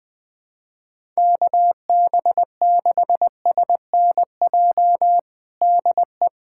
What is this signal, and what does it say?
Morse code, heard as a keyed tone, about 20 wpm, 700 Hz. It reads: KB6SNJ DE
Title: USA Callsign DE Callsign - 6 Characters - 20wpm